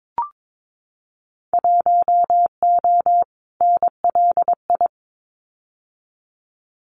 Morse code, heard as a keyed tone, about 22 wpm, 700 Hz. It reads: 1O NLI
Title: Winter Field Day Exchanges 22wpm